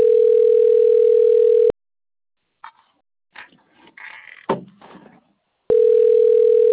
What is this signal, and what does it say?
Ringback tone (North America), 440 + 480 Hz.